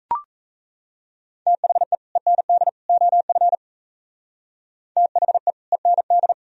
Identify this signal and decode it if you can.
Morse code, heard as a keyed tone, about 33 wpm, 700 Hz. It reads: THE END OF THE END
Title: Speed-Racing - Sets of 3 Words 28wpm